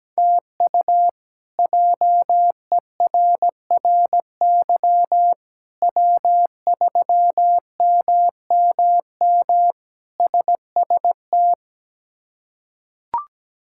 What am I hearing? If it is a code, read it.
Morse code, heard as a keyed tone, about 17 wpm, 700 Hz. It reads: TU JERRY W3MMM SST
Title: K1USN SST QSO Practice - 17wpm